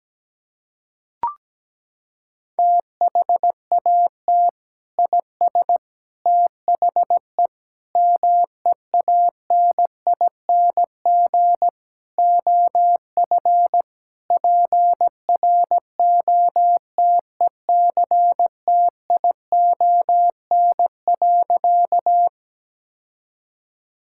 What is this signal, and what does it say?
Morse code, heard as a keyed tone, about 17 wpm, 700 Hz. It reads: THAT IS THE MEANING OF PROTECTION.
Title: Sentences from Top 1600 Words 17wpm